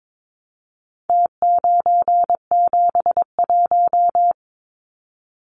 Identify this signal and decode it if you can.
Morse code, heard as a keyed tone, about 22 wpm, 700 Hz. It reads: T971
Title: Numbers - Contest Style - 4-Digit 22wpm